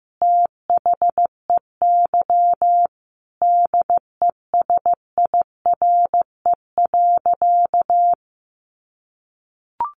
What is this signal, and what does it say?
Morse code, heard as a keyed tone, about 15 wpm, 700 Hz. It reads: THEY DESIRE.
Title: Sentences from Top 1100 Words 15wpm